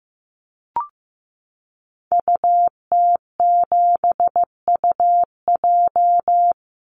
Morse code, heard as a keyed tone, about 15 wpm, 700 Hz. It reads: UT7UJ
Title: Call Signs Repeated 2x - 15wpm